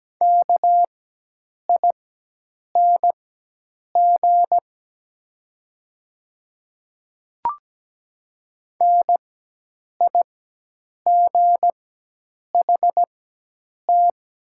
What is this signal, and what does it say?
Morse code, heard as a keyed tone, about 17 wpm, spaced as for 8 wpm, 700 Hz. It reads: KING NIGHT
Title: 4x Character Spacing - Top 500 Words - 17wpm